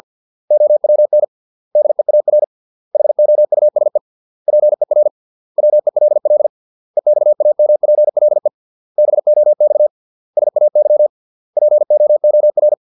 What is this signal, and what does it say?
Morse code, heard as a keyed tone, about 50 wpm, 600 Hz. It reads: OWN DEAR HOUSE PER WELL EXAMPLE BOX SAY POOR